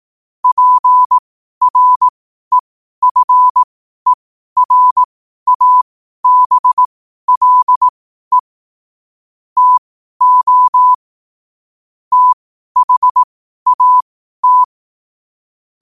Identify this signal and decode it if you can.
Morse code, heard as a keyed tone, about 18 wpm, spaced as for 12 wpm, 1000 Hz. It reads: PREFERABLE TO THAT
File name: qotdfiles / 12wpm